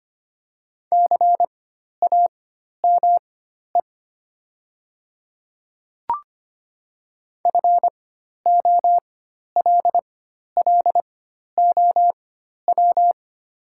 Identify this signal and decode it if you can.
Morse code, heard as a keyed tone, about 25 wpm, spaced as for 12 wpm, 700 Hz. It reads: CAME FOLLOW
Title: 4x Character Spacing - Top 200 Words - 25wpm